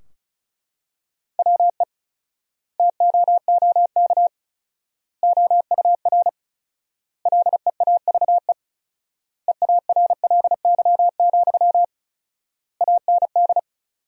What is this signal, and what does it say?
Morse code, heard as a keyed tone, about 34 wpm, 700 Hz. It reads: WE TOOK OUR LEAVE EARLY, AND
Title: 4x Word-Spacing - Sentences from Top 400 Words - 35wpm